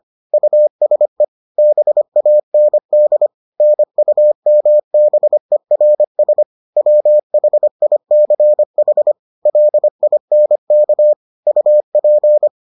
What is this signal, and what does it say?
Morse code, heard as a keyed tone, about 25 wpm, 600 Hz. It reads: USE BAND NUMBERS WHICH LINK UP